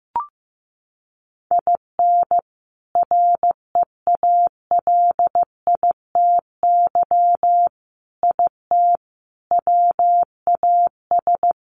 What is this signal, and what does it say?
Morse code, heard as a keyed tone, about 14 wpm, 700 Hz. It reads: IN REALITY IT WAS
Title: Sentences from Top 1200 Words 15wpm